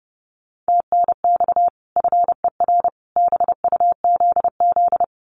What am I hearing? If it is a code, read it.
Morse code, heard as a keyed tone, about 30 wpm, 700 Hz. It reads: TNX FER BUZZ
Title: Common QSO Phrases 30wpm